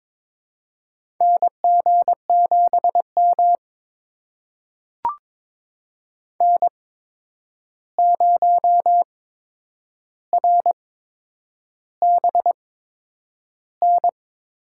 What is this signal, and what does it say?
Morse code, heard as a keyed tone, about 22 wpm, spaced as for 9 wpm, 700 Hz. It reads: NG7M N0RBN
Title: USA Call Signs - 8x Character Spacing - 22wpm